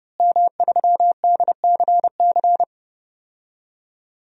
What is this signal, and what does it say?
Morse code, heard as a keyed tone, about 32 wpm, 700 Hz. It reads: M3DCC
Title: DXCC-100 Call Signs - 5 Characters - Repeated 3x - 30wpm